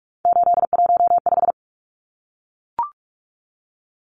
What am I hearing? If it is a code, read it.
Morse code, heard as a keyed tone, about 44 wpm, 700 Hz. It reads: 815
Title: Numbers - 3-Digit 45wpm